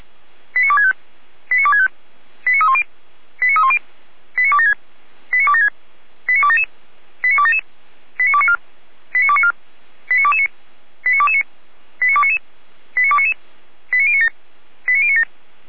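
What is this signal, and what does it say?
Five-tone selective call (ZVEI), heard as quick runs of five tones, tones of about 70 ms. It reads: 89265 89265 89310 89310 89276 89276 89277 89277 89294 89294 89229 89229 89299 89299 89097 89097
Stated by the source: ZVEI five-tone selective call 89250 received by SDR, from the feuerwehr-orlamuende fire brigade audio archive.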